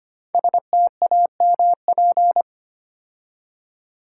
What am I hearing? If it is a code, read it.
Morse code, heard as a keyed tone, about 25 wpm, 700 Hz. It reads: STAMP
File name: 5 Letter Words - First 3 Letters Given - 25wpm